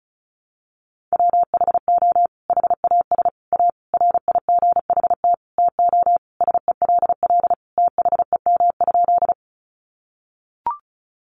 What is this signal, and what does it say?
Morse code, heard as a keyed tone, about 34 wpm, 700 Hz. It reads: WHO HAS A RIGHT TO SELL THEM?
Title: Sentences from Top 1100 Words 35wpm